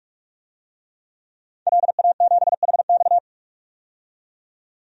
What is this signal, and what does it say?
Morse code, heard as a keyed tone, about 44 wpm, 690 Hz. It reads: LA7HX
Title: DXCC-100 Call Signs - 5 Characters - 45wpm